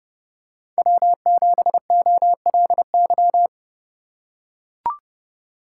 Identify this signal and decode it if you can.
Morse code, heard as a keyed tone, about 30 wpm, 700 Hz. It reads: W7OLY
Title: USA Call Signs - 4x Character Spacing - 30wpm